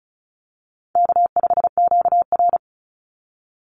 Morse code, heard as a keyed tone, about 34 wpm, 700 Hz. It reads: K5QR
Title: NA SOTA Chaser Callsigns 35wpm